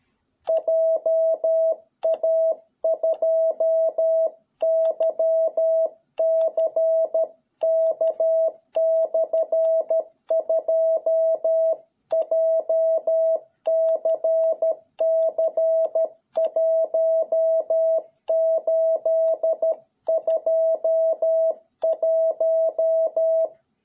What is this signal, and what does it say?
Morse code, heard as a keyed tone, about 13 wpm, 660 Hz. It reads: JA2YCK/2 JCC1821